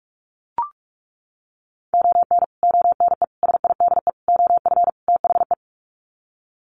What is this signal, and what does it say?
Morse code, heard as a keyed tone, about 44 wpm, 700 Hz. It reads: ON ONE SIDE OF THE
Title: Sets of 5 Words 45wpm